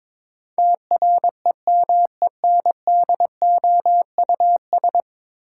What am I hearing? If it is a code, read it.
Morse code, heard as a keyed tone, about 22 wpm, 700 Hz. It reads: TREMENDOUS
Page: morse-code-ninja-practice-22wpm